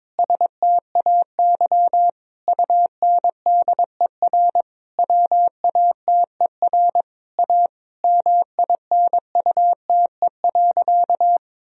Morse code, heard as a keyed tone, about 22 wpm, 700 Hz. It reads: STAY UNDER WATER A MINUTE.
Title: Sentences from Top 1000 Words - Review Part 1 - 22wpm